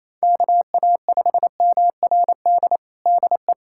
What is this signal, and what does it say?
Morse code, heard as a keyed tone, about 28 wpm, 700 Hz. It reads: KA5MRD DE
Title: USA Callsign DE Callsign 28wpm